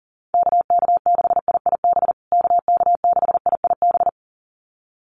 Morse code, heard as a keyed tone, about 37 wpm, 700 Hz. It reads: KK6IIB KK6IIB
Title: USA Call Signs Repeated 2x 40wpm